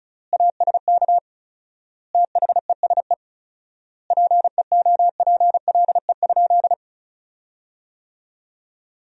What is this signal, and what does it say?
Morse code, heard as a keyed tone, about 34 wpm, 700 Hz. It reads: ASK THESE PEOPLE?
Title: 4x Word-Spacing - Sentences from Top 200 Words - 35wpm